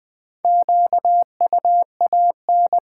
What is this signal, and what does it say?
Morse code, heard as a keyed tone, about 20 wpm, 700 Hz. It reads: QUAN
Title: Build A Word - 4 Letters At A Time - 8-Letter Words - 20wpm